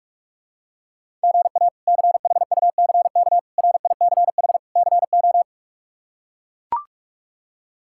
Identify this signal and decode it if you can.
Morse code, heard as a keyed tone, about 42 wpm, 700 Hz. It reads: GA CHUCK RICH CO
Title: K1USN SST QSO Practice - 45wpm